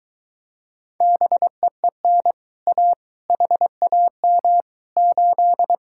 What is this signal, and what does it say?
Morse code, heard as a keyed tone, about 23 wpm, 700 Hz. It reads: BEEN A HAM 8